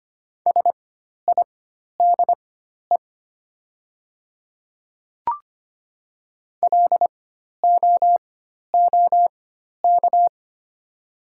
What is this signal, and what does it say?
Morse code, heard as a keyed tone, about 25 wpm, spaced as for 12 wpm, 700 Hz. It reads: SIDE LOOK
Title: 4x Character Spacing - Top 100 Words - 25wpm